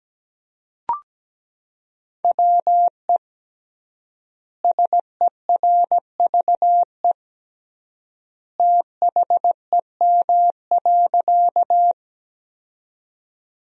Morse code, heard as a keyed tone, about 17 wpm, 700 Hz. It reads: WE SERVE THEM.
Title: 3x Word-Spacing - Sentences from Top 400 Words - 17wpm